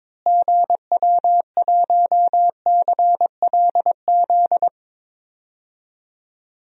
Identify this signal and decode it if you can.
Morse code, heard as a keyed tone, about 22 wpm, 700 Hz. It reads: GW1CLZ
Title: DXCC-100 Call Signs - 6 Characters - Repeated 3x - 22wpm